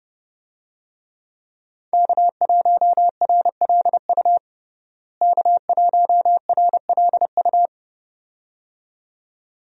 Morse code, heard as a keyed tone, about 30 wpm, 700 Hz. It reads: K1RLU K1RLU
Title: USA Call Signs - 5 Characters - Repeated 2x - 30wpm